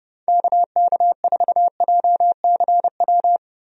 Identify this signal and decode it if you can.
Morse code, heard as a keyed tone, about 30 wpm, 700 Hz. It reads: KK4JCW
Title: USA Call Signs - 6 Characters - Repeated 2x - 30wpm